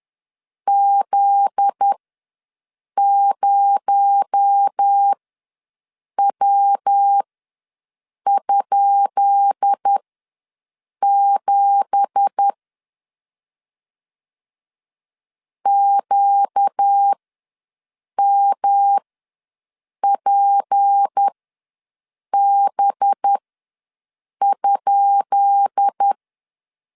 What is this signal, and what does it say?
Morse code, heard as a keyed tone, about 10 wpm, spaced as for 6 wpm, 800 Hz. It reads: Z0W?7 QMPB?
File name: LEZIONI / 10 SIGNS / 10B INT ws 10wpm 4134